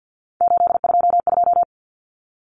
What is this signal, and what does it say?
Morse code, heard as a keyed tone, about 49 wpm, 700 Hz. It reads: 822
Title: Numbers - 3-Digit 50wpm